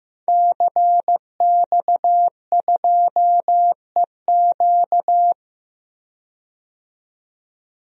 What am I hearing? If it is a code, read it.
Morse code, heard as a keyed tone, about 15 wpm, 700 Hz. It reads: CX2EQ
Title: DXCC-100 Call Signs - 5 Characters - 15wpm